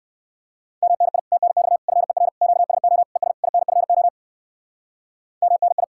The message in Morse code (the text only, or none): MNI TNX FER QSO ES INFO MNI